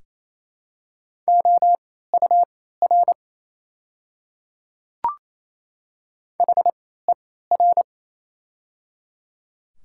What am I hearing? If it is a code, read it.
Morse code, heard as a keyed tone, about 28 wpm, spaced as for 16 wpm, 700 Hz. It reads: OUR HER
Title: 3x Character Spacing - Top 500 Words - 28wpm